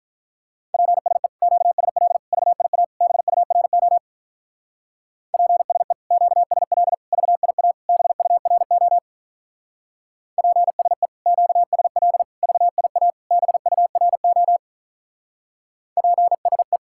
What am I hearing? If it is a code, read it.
Morse code, heard as a keyed tone, about 46 wpm, 700 Hz. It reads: PSE QSL VIA BURO PSE QSL VIA BURO PSE QSL VIA BURO PSE